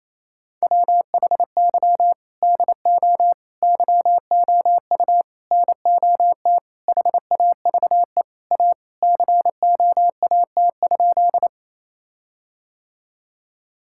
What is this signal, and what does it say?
Morse code, heard as a keyed tone, about 28 wpm, 700 Hz. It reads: WHY DO YOU NOT HAVE A COAT?
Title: Sentences from Top 1600 Words 28wpm